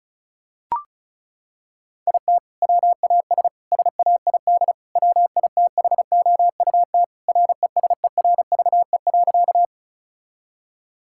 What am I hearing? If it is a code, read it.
Morse code, heard as a keyed tone, about 36 wpm, 700 Hz. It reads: IT WAS SAID WITHOUT RESERVE.
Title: Sentences from Top 1000 Words - Review Part 6 - 35wpm